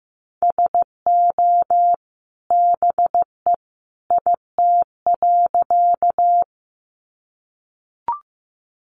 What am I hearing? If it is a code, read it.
Morse code, heard as a keyed tone, about 15 wpm, 700 Hz. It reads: SO BE IT.
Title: Sentences from Top 100 Words 15wpm